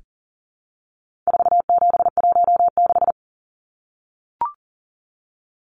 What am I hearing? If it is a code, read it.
Morse code, heard as a keyed tone, about 40 wpm, 700 Hz. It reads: 4716
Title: Numbers - 4-Digit 40wpm